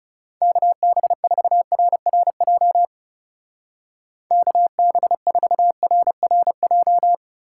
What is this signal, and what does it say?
Morse code, heard as a keyed tone, about 37 wpm, 700 Hz. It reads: KB4RRJ KB4RRJ
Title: Speed Racing - USA 2x3 Call Signs 30wpm